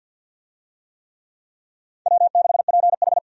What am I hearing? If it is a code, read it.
Morse code, heard as a keyed tone, about 52 wpm, 700 Hz. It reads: W6PH
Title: USA Call Signs - 4 Characters - 50wpm